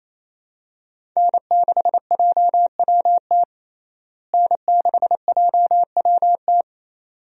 Morse code, heard as a keyed tone, about 28 wpm, 700 Hz. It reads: N6JWT N6JWT
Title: USA Call Signs - 5 Characters - Repeated 2x - 28wpm